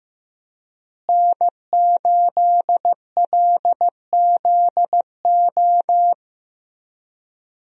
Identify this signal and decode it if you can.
Morse code, heard as a keyed tone, about 16 wpm, 700 Hz. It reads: N8LZO
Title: USA Call Signs - 5 Characters - Repeated 3x - 15wpm